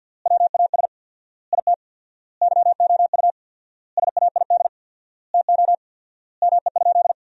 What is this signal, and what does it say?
Morse code, heard as a keyed tone, about 48 wpm, 700 Hz. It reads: WAS IT YOU SAID TO ME?
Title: 4x Word-Spacing - Sentences from Top 200 Words - 50wpm